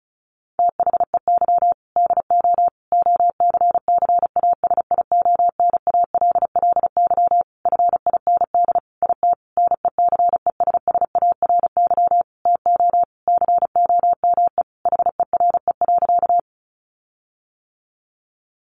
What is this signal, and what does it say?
Morse code, heard as a keyed tone, about 34 wpm, 700 Hz. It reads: THEY DO OCCASIONALLY FIND IT NECESSARY TO COME HERE.
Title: Sentences from Top 1300 Words 35wpm